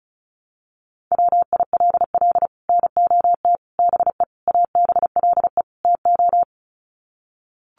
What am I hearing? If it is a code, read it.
Morse code, heard as a keyed tone, about 34 wpm, 700 Hz. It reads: WILL NOT BE ABLE TO
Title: Sets of 5 Words 35wpm